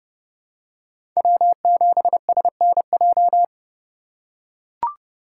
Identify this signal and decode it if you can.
Morse code, heard as a keyed tone, about 30 wpm, 690 Hz. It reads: W7SNJ
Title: USA Call Signs - 5 Characters - 2x Character Spacing - 30wpm